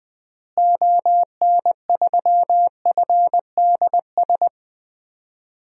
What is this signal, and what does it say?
Morse code, heard as a keyed tone, about 20 wpm, 700 Hz. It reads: ON3FDS
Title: DXCC-100 Call Signs - 6 Characters - Repeated 2x - 20wpm